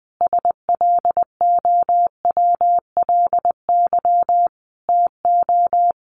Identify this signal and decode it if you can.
Morse code, heard as a keyed tone, about 20 wpm, 700 Hz. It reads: SLOWLY TO
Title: Sentences from Top 1400 Words 20wpm